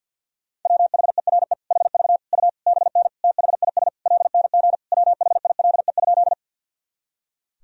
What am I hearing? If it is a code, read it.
Morse code, heard as a keyed tone, about 52 wpm, 700 Hz. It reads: WHERE HV U BN THIS LNG WHILE?
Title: Sentences from Top 300 Words and Abbreviations 50wpm